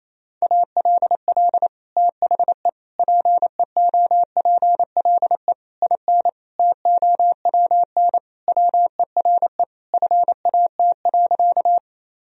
Morse code, heard as a keyed tone, about 28 wpm, 700 Hz. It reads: ALL THE PEOPLE IN TOWN WERE FAT.
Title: Sentences from Top 1000 Words - Review Part 4 - 28wpm